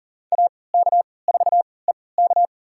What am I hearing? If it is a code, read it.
Morse code, heard as a keyed tone, about 40 wpm, spaced as for 23 wpm, 700 Hz. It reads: AK4EK
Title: USA Call Signs - 3x Character Spacing - 40wpm